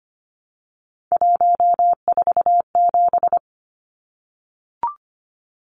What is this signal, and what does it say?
Morse code, heard as a keyed tone, about 25 wpm, 700 Hz. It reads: 147
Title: Numbers - 3-Digit 25wpm